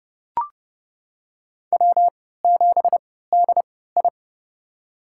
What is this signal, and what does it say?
Morse code, heard as a keyed tone, about 30 wpm, spaced as for 17 wpm, 690 Hz. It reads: W7DI
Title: USA Call Signs - 3x Character Spacing - 30wpm